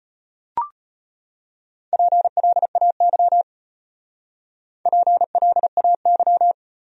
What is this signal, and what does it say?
Morse code, heard as a keyed tone, about 32 wpm, 700 Hz. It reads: PLAY PLAY